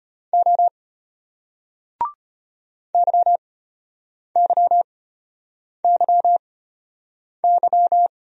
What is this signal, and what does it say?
Morse code, heard as a keyed tone, about 35 wpm, spaced as for 8 wpm, 700 Hz. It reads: O YYYY